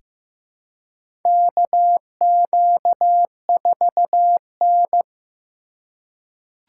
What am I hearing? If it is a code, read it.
Morse code, heard as a keyed tone, about 15 wpm, 700 Hz. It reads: KQ4N
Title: USA Call Signs - 4 Characters - 3x Character Spacing - 15wpm